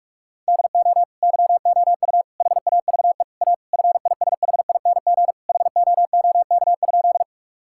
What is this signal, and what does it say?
Morse code, heard as a keyed tone, about 46 wpm, 700 Hz. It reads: DO YOU HAVE A FISHING HOOK?
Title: Sentences from Top 1600 Words 45wpm